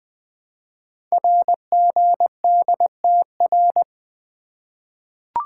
Morse code, heard as a keyed tone, about 20 wpm, 700 Hz. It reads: RGDTR
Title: Code Groups - 5 Letters - 20wpm